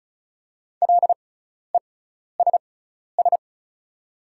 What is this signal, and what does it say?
Morse code, heard as a keyed tone, about 36 wpm, spaced as for 12 wpm, 700 Hz. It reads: LESS
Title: 6x Character Spacing - Top 500 Words - 35wpm